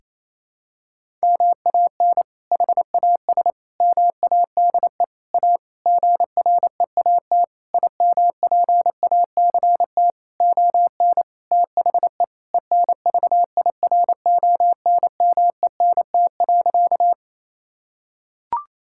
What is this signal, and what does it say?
Morse code, heard as a keyed tone, about 27 wpm, 700 Hz. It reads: MAN HAS MADE A GREAT IMPACT ON THE ENVIRONMENT.
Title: Sentences from Top 1000 Words - Review Part 6 - 28wpm